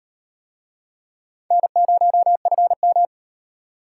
Morse code, heard as a keyed tone, about 38 wpm, 700 Hz. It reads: N0FM